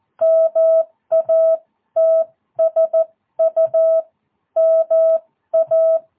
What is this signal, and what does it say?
Morse code, heard as a keyed tone, about 14 wpm, spaced as for 12 wpm, 650 Hz. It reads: MATSU MA
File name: jh8cbh / 2022cw